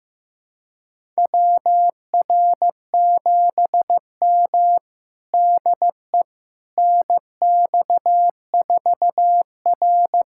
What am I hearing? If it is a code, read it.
Morse code, heard as a keyed tone, about 16 wpm, 700 Hz. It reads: WR7M DE NX4R
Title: USA Callsign DE Callsign - 4 Characters - 15wpm